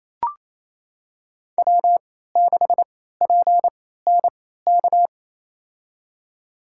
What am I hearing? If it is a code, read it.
Morse code, heard as a keyed tone, about 28 wpm, spaced as for 16 wpm, 700 Hz. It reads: W6PNK
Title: USA Call Signs - 3x Character Spacing - 28wpm